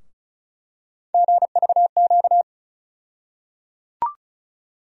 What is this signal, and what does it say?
Morse code, heard as a keyed tone, about 36 wpm, 700 Hz. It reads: GVQ
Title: USA Call Sign Postfixes - 3 Letters 35wpm